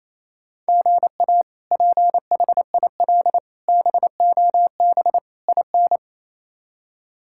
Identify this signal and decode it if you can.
Morse code, heard as a keyed tone, about 28 wpm, 700 Hz. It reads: GA PHIL BOB IN
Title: K1USN SST QSO Practice - 28wpm